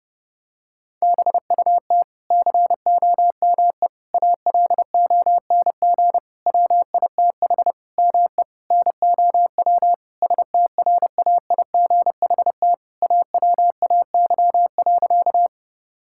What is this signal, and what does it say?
Morse code, heard as a keyed tone, about 28 wpm, 700 Hz. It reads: BUT COME ALONG WITH ME NOW STRAIGHT AWAY.
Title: Sentences from Top 1000 Words - Review Part 6 - 30wpm